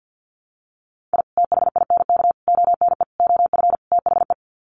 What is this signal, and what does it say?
Morse code, heard as a keyed tone, about 48 wpm, 700 Hz. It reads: I THINK ONE OF THE
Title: Sets of 5 Words 50wpm